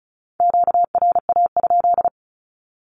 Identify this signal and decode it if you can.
Morse code, heard as a keyed tone, about 36 wpm, 700 Hz. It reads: QRA?
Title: Q-Codes with Spoken Meaning - 35wpm